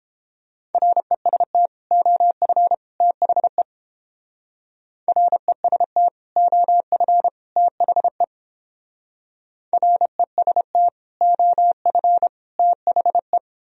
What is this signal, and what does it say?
Morse code, heard as a keyed tone, about 32 wpm, 700 Hz. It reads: REST OF THE REST OF THE REST OF THE